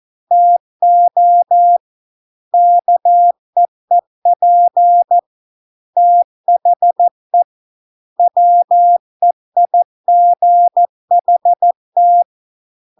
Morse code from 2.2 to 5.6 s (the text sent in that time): KEEP